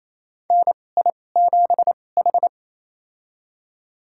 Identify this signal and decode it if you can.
Morse code, heard as a keyed tone, about 28 wpm, spaced as for 20 wpm, 700 Hz. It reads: NI7H
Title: USA Call Signs - 4 Characters - 2x Character Spacing - 28wpm